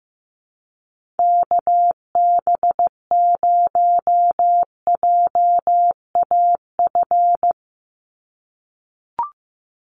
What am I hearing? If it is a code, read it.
Morse code, heard as a keyed tone, about 15 wpm, 700 Hz. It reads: KB0JAF
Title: USA Call Signs 15wpm